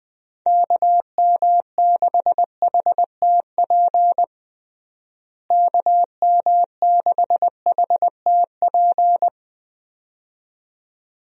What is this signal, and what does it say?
Morse code, heard as a keyed tone, about 20 wpm, 700 Hz. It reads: KM6HTP KM6HTP
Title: USA Call Signs - 6 Characters - Repeated 2x - 20wpm